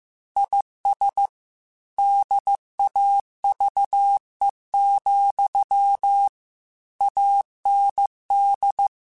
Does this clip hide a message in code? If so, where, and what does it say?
Yes: Morse code from 0.4 to 8.9 s, heard as a keyed tone, about 15 wpm, 800 Hz. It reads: IS DAVE, AND